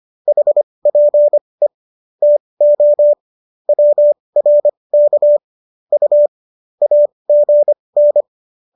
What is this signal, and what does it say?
Morse code, heard as a keyed tone, about 25 wpm, spaced as for 20 wpm, 590 Hz. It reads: HPE TO WRK U AGN